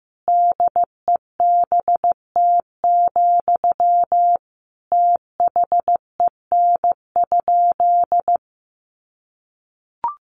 Morse code, heard as a keyed tone, about 15 wpm, 700 Hz. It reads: DEBT, THEN?
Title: Sentences from Top 1300 Words 15wpm